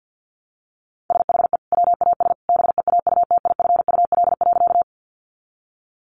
Morse code, heard as a keyed tone, about 50 wpm, 700 Hz. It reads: SHE WAS BEAUTIFUL.
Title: Sentences from Top 900 Words 50wpm